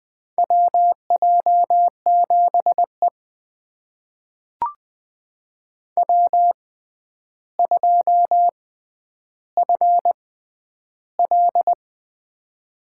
Morse code, heard as a keyed tone, about 20 wpm, spaced as for 10 wpm, 700 Hz. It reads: WJ7E W2FL
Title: USA Call Signs - 4 Characters - 6x Character Spacing - 20wpm